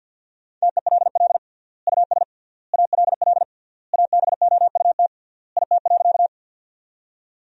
Morse code, heard as a keyed tone, about 52 wpm, 700 Hz. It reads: TELL US ALL ABOUT IT.
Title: 3x Word-Spacing - Sentences from Top 200 Words - 50wpm